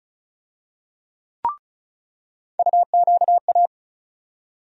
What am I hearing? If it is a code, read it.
Morse code, heard as a keyed tone, about 36 wpm, 700 Hz. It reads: UQA
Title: USA Call Sign Postfixes - 3 Letters 35wpm